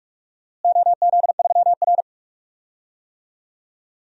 Morse code, heard as a keyed tone, about 42 wpm, 700 Hz. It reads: OZ3R
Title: DXCC-100 Call Signs - 4 Characters - Repeated 3x - 45wpm